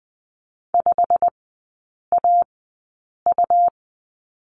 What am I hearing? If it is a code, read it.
Morse code, heard as a keyed tone, about 20 wpm, spaced as for 8 wpm, 700 Hz. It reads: 5AU
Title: 100 Lesson 17e - Letter u - Three Characters 20wpm